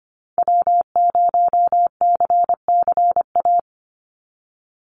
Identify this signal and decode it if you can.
Morse code, heard as a keyed tone, about 25 wpm, 700 Hz. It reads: W0CCA
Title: NA SOTA Activator Callsigns 25wpm